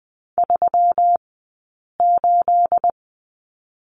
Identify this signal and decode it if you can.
Morse code, heard as a keyed tone, about 20 wpm, spaced as for 8 wpm, 700 Hz. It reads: 38
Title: 211 Lesson 34d - Number 8 - Two Characters 20wpm